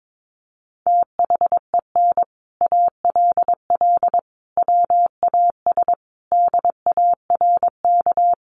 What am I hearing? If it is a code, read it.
Morse code, heard as a keyed tone, about 22 wpm, 700 Hz. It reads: THEN ALL WAS DARK
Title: Sentences from Top 500 Words 22wpm